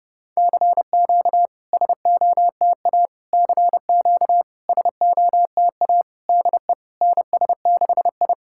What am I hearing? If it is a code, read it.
Morse code, heard as a keyed tone, about 32 wpm, 700 Hz. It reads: CQ SOTA CQ SOTA DE NS6I
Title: SOTA QSO Practice 30wpm